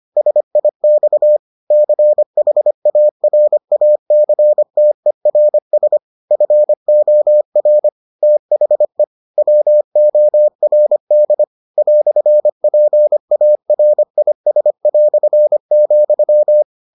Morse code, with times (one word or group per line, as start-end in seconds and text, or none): 0.2-1.4 SIX
1.7-6.0 CHARACTERS
6.3-7.9 FOR
8.2-9.0 THE
9.4-11.4 WORD
11.8-16.6 "PARIS",